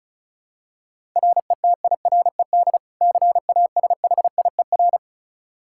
Morse code, heard as a keyed tone, about 36 wpm, 700 Hz. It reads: RETIRED CASHIER
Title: Practice QSO - Exchange 3 - More Info - 35wpm